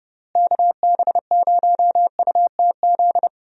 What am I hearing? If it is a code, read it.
Morse code, heard as a keyed tone, about 30 wpm, 700 Hz. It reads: KB0UTZ
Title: USA Call Signs - 6 Characters - Repeated 3x - 30wpm